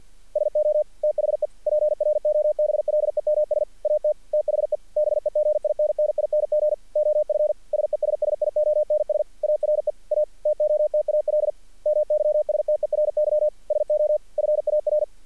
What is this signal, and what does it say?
Morse code, heard as a keyed tone, about 48 wpm, 600 Hz. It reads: SO THE PROBLEMS AT THE BEGINNING OF SESSIONS ARE A TOTAL MYSTERY SO FAR
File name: archived / CRSnet